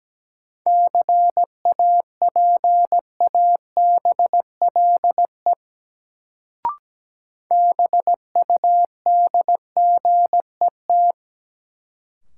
Morse code, heard as a keyed tone, about 17 wpm, 700 Hz. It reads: CAPABLE BUDGET
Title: Sentences from Top 1200 Words - New Word Drill - 17wpm